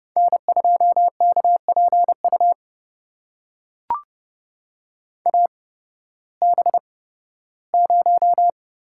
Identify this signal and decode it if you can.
Morse code, heard as a keyed tone, about 30 wpm, spaced as for 16 wpm, 700 Hz. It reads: N2KPU AB0
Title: USA Call Signs - 5 Characters - 8x Character Spacing - 30wpm